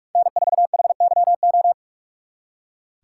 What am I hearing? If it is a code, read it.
Morse code, heard as a keyed tone, about 56 wpm, 700 Hz. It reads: N4HYO
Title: Speed Racing - USA 1x3 Call Signs 45wpm